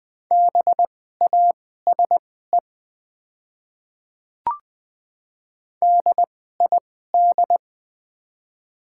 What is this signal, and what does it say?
Morse code, heard as a keyed tone, about 20 wpm, spaced as for 14 wpm, 690 Hz. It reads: BASE DID
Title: 2x Character Spacing - Top 500 Words - 20wpm